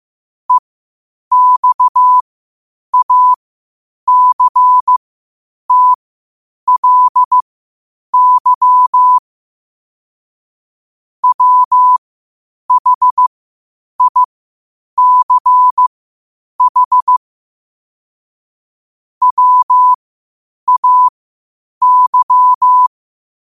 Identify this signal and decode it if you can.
Morse code, heard as a keyed tone, about 15 wpm, spaced as for 8 wpm, 1000 Hz. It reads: EXACTLY WHICH WAY
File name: qotdfiles / 8wpm